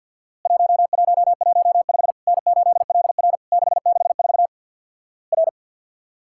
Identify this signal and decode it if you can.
Morse code, heard as a keyed tone, about 50 wpm, 700 Hz. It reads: 1115 N8LR 664 R
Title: Worked All Europe DX Contest - QTC Exchange with Repeats - 50wpm